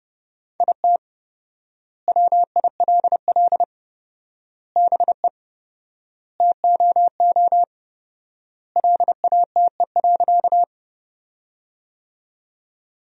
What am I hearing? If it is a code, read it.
Morse code, heard as a keyed tone, about 30 wpm, 700 Hz. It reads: IT WILL BE TOO LATE.
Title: 4x Word-Spacing - Sentences from Top 300 Words - 30wpm